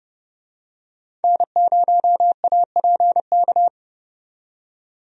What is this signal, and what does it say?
Morse code, heard as a keyed tone, about 30 wpm, 700 Hz. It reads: N0APK
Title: USA Call Signs - 5 Characters - 30wpm